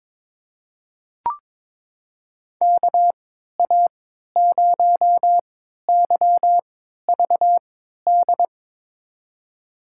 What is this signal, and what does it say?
Morse code, heard as a keyed tone, about 22 wpm, spaced as for 13 wpm, 700 Hz. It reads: KA0YVD
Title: USA Call Signs - 3x Character Spacing - 22wpm